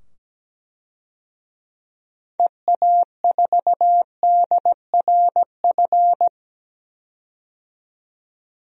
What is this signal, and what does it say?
Morse code, heard as a keyed tone, about 17 wpm, 700 Hz. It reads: EA4DRF
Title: DXCC-100 Call Signs - 6 Characters - 17wpm